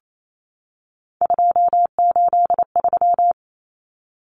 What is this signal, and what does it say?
Morse code, heard as a keyed tone, about 28 wpm, 700 Hz. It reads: 283
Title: Common Cut Numbers - 3-Digit 28wpm